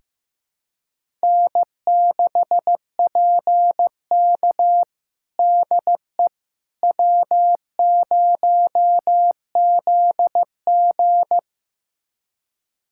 Morse code, heard as a keyed tone, about 15 wpm, 700 Hz. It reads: N6PK DE W0ZG
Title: USA Callsign DE Callsign - 4 Characters - 15wpm